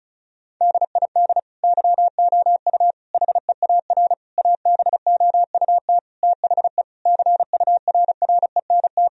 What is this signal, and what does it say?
Morse code, heard as a keyed tone, about 34 wpm, 700 Hz. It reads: DID YOU HEAR ABOUT THE CURRENT
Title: Sentences from Top 1000 Words - Review Part 4 - 35wpm